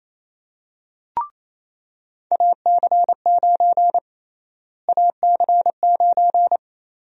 Morse code, heard as a keyed tone, about 27 wpm, 700 Hz. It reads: AC9 AC9
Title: USA Call Sign Prefixes - Repeated 2x - 28wpm